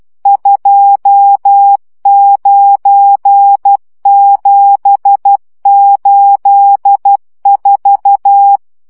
Morse code, heard as a keyed tone, about 12 wpm, 800 Hz. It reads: 29784
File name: morze vizsga 2011-3 10